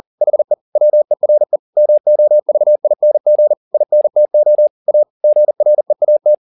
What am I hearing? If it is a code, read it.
Morse code, heard as a keyed tone, about 40 wpm, 600 Hz. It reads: HE WERE MOVING INTO A GREAT
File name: the old man and the sea 2 40wpm